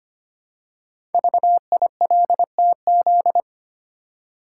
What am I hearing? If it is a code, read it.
Morse code, heard as a keyed tone, about 25 wpm, 700 Hz. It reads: VILTZ
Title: Code Groups - 5 Letters - 25wpm